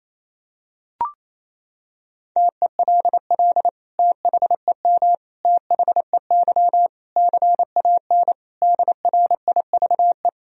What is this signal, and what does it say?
Morse code, heard as a keyed tone, about 27 wpm, 700 Hz. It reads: TELL THEM THEY CAN DRIVE
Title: Sentences from Top 1000 Words - Review Part 2 - 28wpm